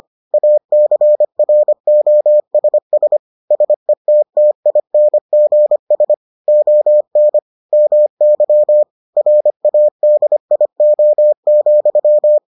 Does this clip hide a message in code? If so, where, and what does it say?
Yes: Morse code from 0.3 to 12.4 s, heard as a keyed tone, about 25 wpm, 600 Hz. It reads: ACROSS SETTINGS ON MY RADIO,